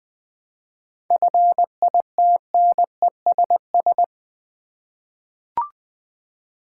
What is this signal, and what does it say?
Morse code, heard as a keyed tone, about 20 wpm, 700 Hz. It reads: FITNESS
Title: 7 Letter Words - First 3 Letters Given - 20wpm